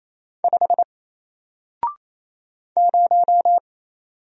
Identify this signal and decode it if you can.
Morse code, heard as a keyed tone, about 27 wpm, 700 Hz. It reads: 5 0
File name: Single-Letter-Number-28wpm